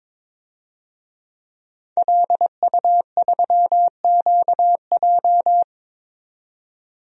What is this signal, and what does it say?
Morse code, heard as a keyed tone, about 22 wpm, 700 Hz. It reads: LU3QJ
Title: DXCC-100 Call Signs - 5 Characters - 22wpm